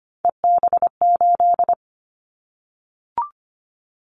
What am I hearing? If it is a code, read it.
Morse code, heard as a keyed tone, about 25 wpm, 700 Hz. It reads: EB8
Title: Callsign Prefixes for Top 100 DXCC - 25wpm